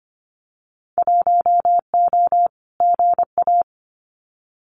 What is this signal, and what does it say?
Morse code, heard as a keyed tone, about 25 wpm, 700 Hz. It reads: 1O GA
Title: Winter Field Day Exchanges 25wpm